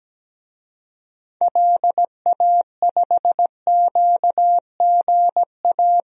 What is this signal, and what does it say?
Morse code, heard as a keyed tone, about 17 wpm, 700 Hz. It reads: LA5QGA
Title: DXCC-100 Call Signs - 6 Characters - Repeated 2x - 17wpm